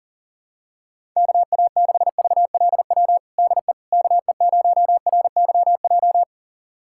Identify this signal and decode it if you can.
Morse code, heard as a keyed tone, about 40 wpm, 700 Hz. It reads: KA6VLW DE KE0RYJ
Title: USA Callsign DE Callsign - 6 Characters - 40wpm